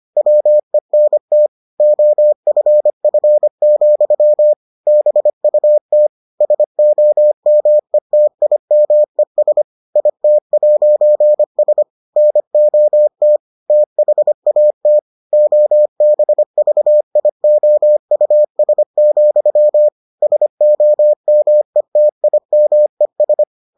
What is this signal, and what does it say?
Morse code, heard as a keyed tone, about 25 wpm, 600 Hz. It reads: WENT OFF, BUT SOMETIMES IT'S NOT THAT OBVIOUS, SOMETIMES